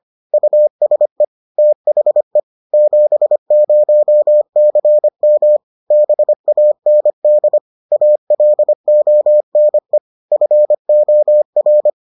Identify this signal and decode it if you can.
Morse code, heard as a keyed tone, about 25 wpm, 600 Hz. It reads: USE THE 70CM BAND ALONE FOR